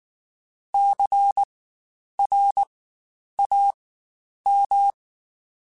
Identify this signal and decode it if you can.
Morse code, heard as a keyed tone, about 19 wpm, spaced as for 9 wpm, 800 Hz. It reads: CRAM
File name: Spaced Alphas / Spaced Alphas mp3s / Cs sp @18